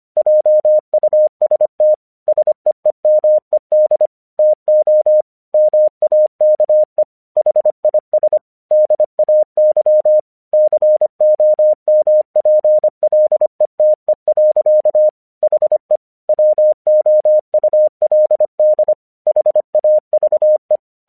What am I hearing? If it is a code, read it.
Morse code, heard as a keyed tone, about 25 wpm, 610 Hz. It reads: JUST SEEMED TO MAKE HIS DAY COMPLETE. HE WOULD HAVE
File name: 1 English 25wpm / AQC32